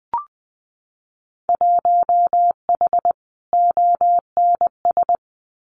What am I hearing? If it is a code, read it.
Morse code, heard as a keyed tone, about 20 wpm, 700 Hz. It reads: 1H ONS
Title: Winter Field Day Exchanges 20wpm